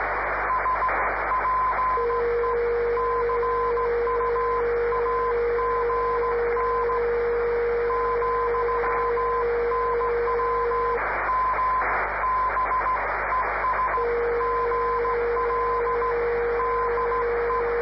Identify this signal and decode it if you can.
Morse code, heard as a keyed tone, about 15 wpm, 1020 Hz, with radio noise. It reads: S PE CUAGN, NW M B ES GL DE
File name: 500khz / graberji swl / g3kev